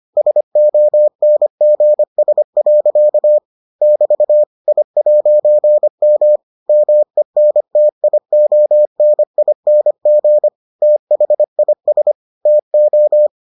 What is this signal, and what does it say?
Morse code, heard as a keyed tone, about 25 wpm, 600 Hz. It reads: SONGS. <BT> I'M MENTIONING THIS TO